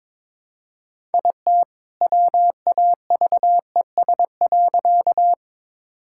Morse code, heard as a keyed tone, about 21 wpm, 700 Hz. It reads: IT WAVES.
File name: All-Age Appropriate Jokes - 22wpm